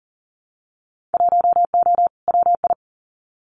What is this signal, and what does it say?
Morse code, heard as a keyed tone, about 38 wpm, 700 Hz. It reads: 1O WI
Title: Winter Field Day Exchanges Encore 40wpm